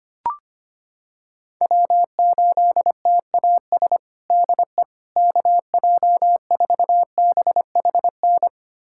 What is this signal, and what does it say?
Morse code, heard as a keyed tone, about 25 wpm, 700 Hz. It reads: W8TAS DE KJ4BHN
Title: USA Callsign DE Callsign 25wpm